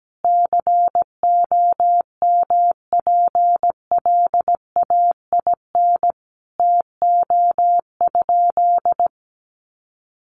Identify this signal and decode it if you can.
Morse code, heard as a keyed tone, about 17 wpm, 700 Hz. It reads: COMPLAIN TO?
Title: Sentences from Top 1300 Words 17wpm